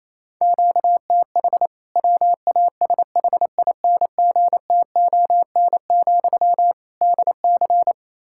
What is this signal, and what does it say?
Morse code, heard as a keyed tone, about 28 wpm, 700 Hz. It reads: QTH WASHINGTON, DC
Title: K1USN SST QSO Practice 28wpm